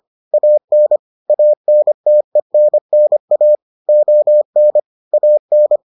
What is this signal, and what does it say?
Morse code, heard as a keyed tone, about 25 wpm, 600 Hz. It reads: AN ANTENNA ON AN